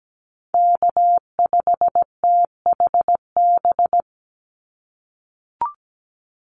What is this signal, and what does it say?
Morse code, heard as a keyed tone, about 17 wpm, 700 Hz. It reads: K5THB
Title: USA Call Signs 17wpm